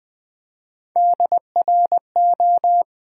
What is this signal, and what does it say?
Morse code, heard as a keyed tone, about 20 wpm, 700 Hz. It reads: DRO
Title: Build A Word - 1 Letter At A Time - 5-Letter Words - 20wpm